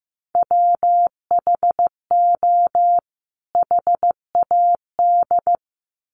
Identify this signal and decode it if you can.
Morse code, heard as a keyed tone, about 15 wpm, 700 Hz. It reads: WHO HAD
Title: Sets of 2 Words 15wpm